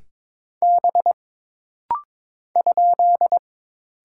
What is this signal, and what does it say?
Morse code, heard as a keyed tone, about 22 wpm, 700 Hz. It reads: B ?
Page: This rapid-fire timing is geared to push your ICR (Instant Character Recognition) skills.